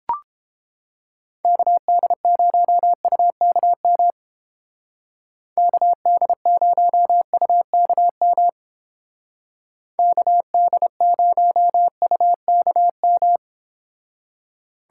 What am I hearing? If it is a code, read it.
Morse code, heard as a keyed tone, about 29 wpm, 700 Hz. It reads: KD0UKM KD0UKM KD0UKM